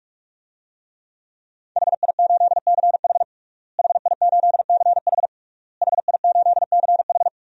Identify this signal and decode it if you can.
Morse code, heard as a keyed tone, about 42 wpm, 700 Hz. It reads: HI8CH HI8CH HI8CH
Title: DXCC-100 Call Signs - 5 Characters - Repeated 3x - 45wpm